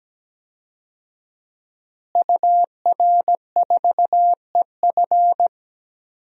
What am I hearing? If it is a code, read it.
Morse code, heard as a keyed tone, about 17 wpm, 700 Hz. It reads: UR4EF
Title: DXCC-100 Call Signs - 5 Characters - Repeated 2x - 17wpm